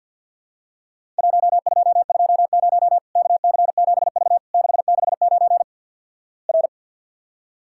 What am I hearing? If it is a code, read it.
Morse code, heard as a keyed tone, about 53 wpm, 700 Hz. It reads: 1220 KC7V 669 R
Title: Worked All Europe DX Contest - QTC Exchange with Repeats - 50wpm